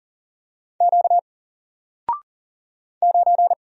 Morse code, heard as a keyed tone, about 40 wpm, 700 Hz. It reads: Q 9
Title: Single Letter-Number 40wpm